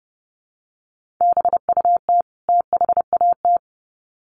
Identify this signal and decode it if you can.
Morse code, heard as a keyed tone, about 30 wpm, 700 Hz. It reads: BUT THAT
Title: Sets of 2 Words 30wpm